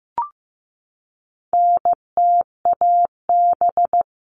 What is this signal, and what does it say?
Morse code, heard as a keyed tone, about 15 wpm, 700 Hz. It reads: NTAB
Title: All Cut Numbers - 4-Digit 15wpm